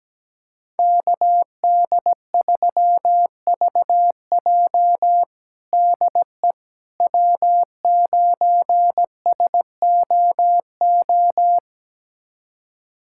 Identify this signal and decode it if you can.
Morse code, heard as a keyed tone, about 17 wpm, 700 Hz. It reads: KD3VJ DE W9SOO
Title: USA Callsign DE Callsign - 5 Characters - 17wpm